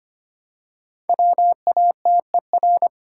Morse code, heard as a keyed tone, about 25 wpm, 700 Hz. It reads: WATER
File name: 5 Letter Words - First 3 Letters Given - 25wpm